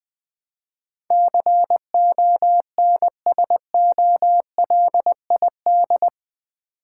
Morse code, heard as a keyed tone, about 20 wpm, 700 Hz. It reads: CONSOLID
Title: Build A Word - 4 Letters At A Time - 11-Letter Words - 20wpm